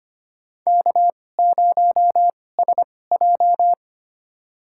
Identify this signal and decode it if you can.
Morse code, heard as a keyed tone, about 25 wpm, spaced as for 18 wpm, 700 Hz. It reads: K0SJ
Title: USA Call Signs - 4 Characters - 2x Character Spacing - 25wpm